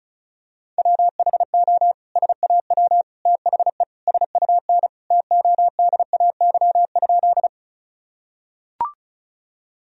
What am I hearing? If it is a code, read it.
Morse code, heard as a keyed tone, about 34 wpm, 700 Hz. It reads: WHO SAW THE SUN TODAY?
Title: Sentences from Top 1000 Words - Review Part 1 - 35wpm